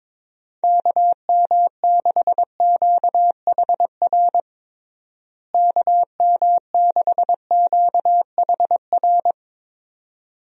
Morse code, heard as a keyed tone, about 22 wpm, 700 Hz. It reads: KM6QHR KM6QHR
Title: USA Call Signs - 6 Characters - Repeated 2x - 22wpm